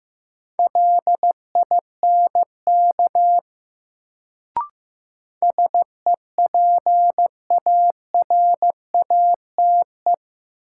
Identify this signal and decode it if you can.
Morse code, heard as a keyed tone, about 15 wpm, 700 Hz. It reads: LINK SEPARATE
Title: Sentences from Top 1000 Words - Review Part 7 - New Word Drill - 15wpm